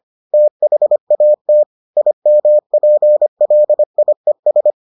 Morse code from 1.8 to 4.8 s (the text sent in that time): IMPLIES